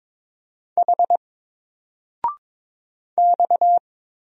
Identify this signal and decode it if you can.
Morse code, heard as a keyed tone, about 22 wpm, 700 Hz. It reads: H X
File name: Single-Letter-Number-22wpm